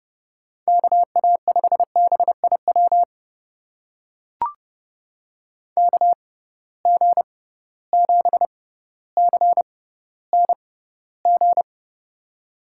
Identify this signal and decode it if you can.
Morse code, heard as a keyed tone, about 31 wpm, spaced as for 16 wpm, 700 Hz. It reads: KA5BIW KG7CNG
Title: USA Call Signs - 6 Characters - 6x Character Spacing - 30wpm